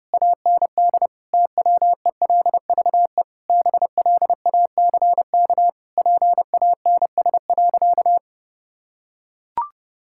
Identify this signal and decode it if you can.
Morse code, heard as a keyed tone, about 32 wpm, 700 Hz. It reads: AND TWELVE BLACK PANS.
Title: Pangram Challenge v2 - 30wpm